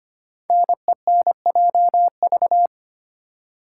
Morse code, heard as a keyed tone, about 25 wpm, 700 Hz. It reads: NENJV
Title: Code Groups - 5 Letters - 25wpm